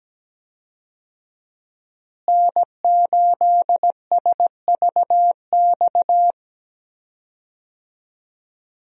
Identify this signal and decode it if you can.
Morse code, heard as a keyed tone, about 17 wpm, 700 Hz. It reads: N8SVX
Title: USA Call Signs - 5 Characters - Repeated 2x - 17wpm